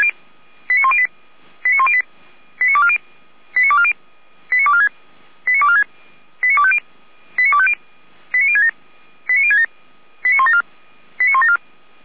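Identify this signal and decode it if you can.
Five-tone selective call (ZVEI), heard as quick runs of five tones, tones of about 70 ms. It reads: 89108 89108 89240 89240 89246 89246 89250 89250 89067 89067 89174 89174
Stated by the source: ZVEI five-tone call received on SDR by the Orlamünde fire brigade (feuerwehr-orlamuende).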